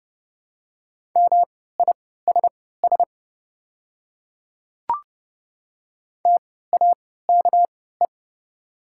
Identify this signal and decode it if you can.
Morse code, heard as a keyed tone, about 30 wpm, spaced as for 17 wpm, 700 Hz. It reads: MISS TAKE
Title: 3x Character Spacing - Top 500 Words - 30wpm